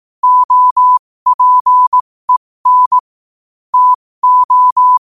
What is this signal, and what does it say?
Morse code, heard as a keyed tone, about 18 wpm, spaced as for 15 wpm, 1000 Hz. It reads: OPEN TO
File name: qotdfiles / 15wpm